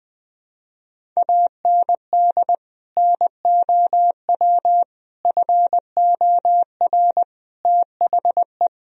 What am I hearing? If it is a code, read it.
Morse code, heard as a keyed tone, about 20 wpm, 700 Hz. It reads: AND NOW FOR THE
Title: Sentences from Top 1000 Words - Review Part 1 - 20wpm